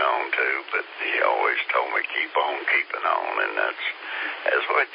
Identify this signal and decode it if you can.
Emergency Alert System (SAME) tones, heard as harsh buzzing bursts, with speech. It is heard from 0.3 to 4.0 s.